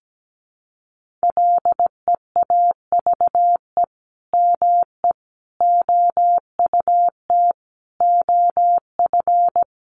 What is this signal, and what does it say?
Morse code, heard as a keyed tone, about 17 wpm, 700 Hz. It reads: LEAVE ME OUT OF
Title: Sentences from Top 400 Words 17wpm